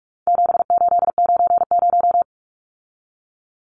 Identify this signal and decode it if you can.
Morse code, heard as a keyed tone, about 44 wpm, 700 Hz. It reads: 7890
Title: Numbers - 4-Digit 45wpm